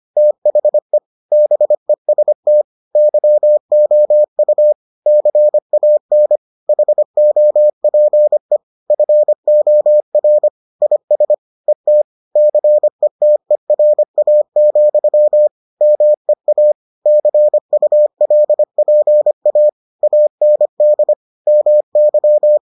Morse code, heard as a keyed tone, about 25 wpm, 600 Hz. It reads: THE BEST YOU CAN HOPE FOR IS ET CETERA, MEA CULPA AND MY